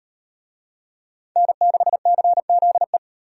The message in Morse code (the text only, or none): N6CZE